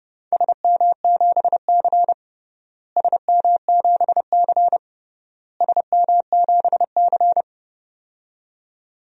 Morse code, heard as a keyed tone, about 29 wpm, 700 Hz. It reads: SM7C SM7C SM7C
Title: DXCC-100 Call Signs - 4 Characters - Repeated 3x - 30wpm